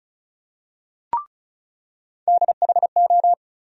Morse code, heard as a keyed tone, about 36 wpm, 700 Hz. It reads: DHO
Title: USA Call Sign Postfixes - 3 Letters 35wpm